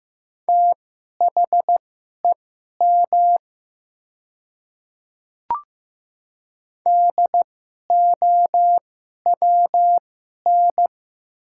Morse code, heard as a keyed tone, about 15 wpm, spaced as for 11 wpm, 690 Hz. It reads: THEM DOWN
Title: 2x Character Spacing - Top 100 Words - 15wpm